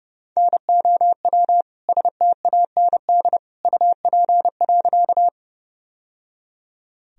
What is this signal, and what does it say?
Morse code, heard as a keyed tone, about 32 wpm, 700 Hz. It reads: NOW STAND UP.
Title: Sentences from Top 1000 Words - Review Part 1 - 30wpm